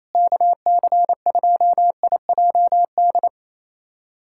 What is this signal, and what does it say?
Morse code, heard as a keyed tone, about 27 wpm, 700 Hz. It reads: KC2IJD
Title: USA Callsign DE Callsign - 6 Characters - 28wpm